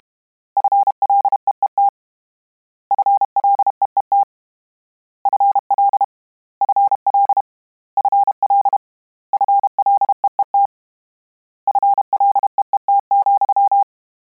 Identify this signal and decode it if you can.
Morse code, heard as a keyed tone, about 32 wpm, 800 Hz. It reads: FLEET FLEET FL FL FL FLEET FLEET,
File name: Combinations / Combinations-mp3s / Combinations@30